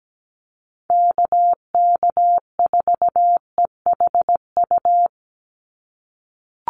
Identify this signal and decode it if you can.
Morse code, heard as a keyed tone, about 17 wpm, 700 Hz. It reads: KK4EHU
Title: USA Call Signs 17wpm